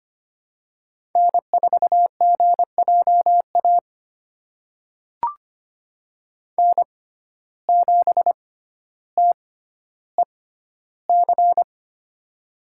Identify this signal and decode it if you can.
Morse code, heard as a keyed tone, about 25 wpm, spaced as for 13 wpm, 700 Hz. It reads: N4GJA N7TEC
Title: USA Call Signs - 5 Characters - 6x Character Spacing - 25wpm